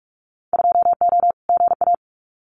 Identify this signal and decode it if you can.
Morse code, heard as a keyed tone, about 46 wpm, 700 Hz. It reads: 2O GA
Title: Winter Field Day Exchanges Encore 45wpm